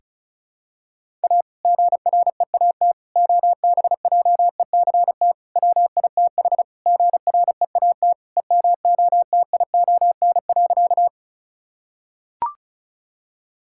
Morse code, heard as a keyed tone, about 36 wpm, 700 Hz. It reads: A GREAT OBJECT WITH GREAT EMOTION.
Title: Sentences from Top 1600 Words 35wpm